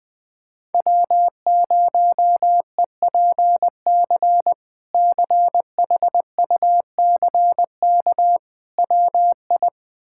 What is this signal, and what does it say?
Morse code, heard as a keyed tone, about 20 wpm, 700 Hz. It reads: W0EPC CHUCK WI
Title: K1USN SST QSO Practice - 20wpm